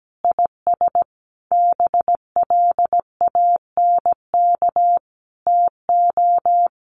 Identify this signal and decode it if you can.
Morse code, heard as a keyed tone, about 17 wpm, 700 Hz. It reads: IS BLANK TO
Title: Sentences from Top 1500 Words 17wpm